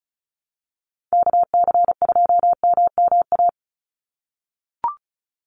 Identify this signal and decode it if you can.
Morse code, heard as a keyed tone, about 34 wpm, 700 Hz. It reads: KC2MMA
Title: USA Call Signs 35wpm